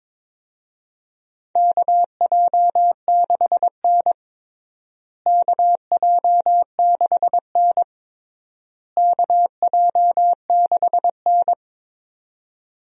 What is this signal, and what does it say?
Morse code, heard as a keyed tone, about 22 wpm, 700 Hz. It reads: KJ6N KJ6N KJ6N
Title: USA Call Signs - 4 Characters - Repeated 3x - 22wpm